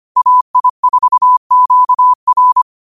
Morse code, heard as a keyed tone, about 25 wpm, 1000 Hz. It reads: AI4QR
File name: qotdfiles / 25wpm